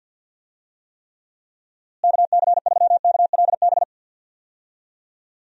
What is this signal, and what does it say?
Morse code, heard as a keyed tone, about 53 wpm, 700 Hz. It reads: KC3KLB
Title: USA Call Signs - 6 Characters - Repeated 3x - 50wpm